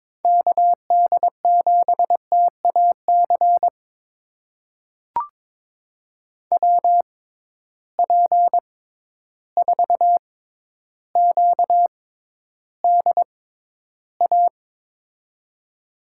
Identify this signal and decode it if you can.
Morse code, heard as a keyed tone, about 22 wpm, spaced as for 11 wpm, 700 Hz. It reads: KD7TAC WP4QDA
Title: USA Call Signs - 6 Characters - 6x Character Spacing - 22wpm